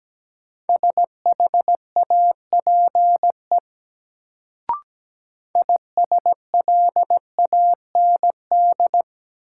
Morse code, heard as a keyed tone, about 17 wpm, 700 Hz. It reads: SHAPE ISLAND
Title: Sentences from Top 500 Words - New Word Drill - 17wpm